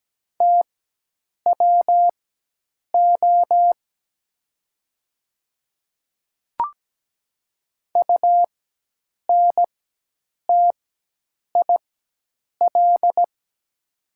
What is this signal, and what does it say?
Morse code, heard as a keyed tone, about 17 wpm, spaced as for 8 wpm, 700 Hz. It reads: TWO UNTIL
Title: 4x Character Spacing - Top 300 Words - 17wpm